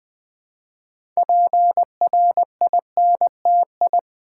Morse code, heard as a keyed tone, about 20 wpm, 700 Hz. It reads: PRINTI
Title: Build A Word - 3 Letters At A Time - 8-Letter Words - 20wpm